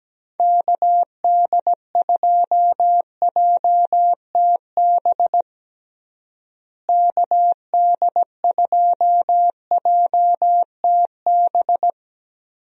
Morse code, heard as a keyed tone, about 17 wpm, 700 Hz. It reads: KD2JTB KD2JTB
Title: USA Call Signs - 6 Characters - Repeated 3x - 17wpm